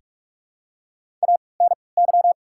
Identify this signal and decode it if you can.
Morse code, heard as a keyed tone, about 48 wpm, 700 Hz. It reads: A N Y
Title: 3x Character Spacing - Top 300 Words - 45wpm